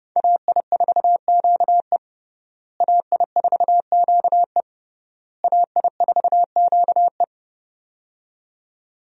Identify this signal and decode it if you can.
Morse code, heard as a keyed tone, about 32 wpm, 700 Hz. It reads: AI4QE AI4QE AI4QE
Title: USA Call Signs - 5 Characters - Repeated 3x - 30wpm